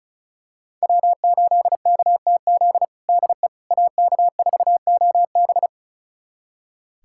Morse code, heard as a keyed tone, about 36 wpm, 700 Hz. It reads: W8KTZ DE AK4OB
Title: USA Callsign DE Callsign - 5 Characters - 35wpm